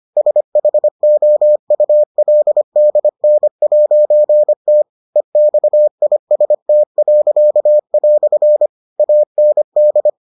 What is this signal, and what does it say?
Morse code, heard as a keyed tone, about 25 wpm, 600 Hz. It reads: SHOULDN'T EXIST." AND